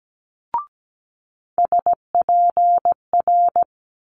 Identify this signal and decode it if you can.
Morse code, heard as a keyed tone, about 17 wpm, 700 Hz. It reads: SPR
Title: Multi-letter Phonemes Quick Fire 17wpm